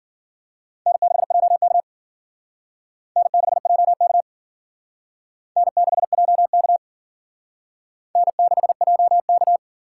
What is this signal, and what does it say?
Morse code, heard as a keyed tone, about 54 wpm, 700 Hz. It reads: N6JK N6JK N6JK N6JK